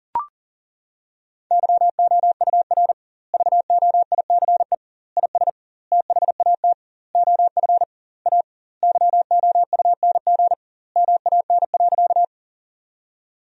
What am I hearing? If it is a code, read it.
Morse code, heard as a keyed tone, about 40 wpm, 700 Hz. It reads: YOUR VOICE IS THAT OF A YOUNG MAN.
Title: 2x Word-Spacing - Sentences from Top 500 Words - 40wpm